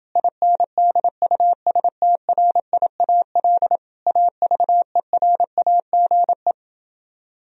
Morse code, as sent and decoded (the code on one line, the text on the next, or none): .. -. -.. ..- ... - .-. .. .- .-.. / .- ...- . .-. .- --. .
INDUSTRIAL AVERAGE